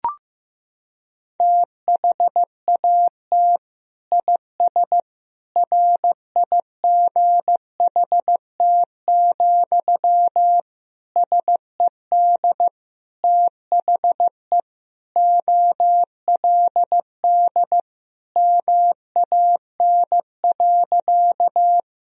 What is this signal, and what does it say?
Morse code, heard as a keyed tone, about 15 wpm, 700 Hz. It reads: THAT IS RIGHT, SED THE OLD MAN.
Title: Sentences from Top 200 Words and Abbreviations 15wpm